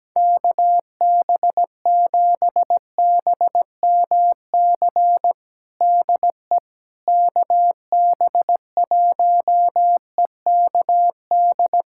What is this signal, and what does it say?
Morse code, heard as a keyed tone, about 17 wpm, 700 Hz. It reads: KB7BMC DE KB1EKD
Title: USA Callsign DE Callsign - 6 Characters - 17wpm